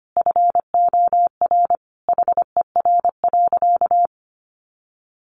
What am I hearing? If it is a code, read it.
Morse code, heard as a keyed tone, about 25 wpm, 700 Hz. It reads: FOR HER.
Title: Sentences from Top 900 Words 25wpm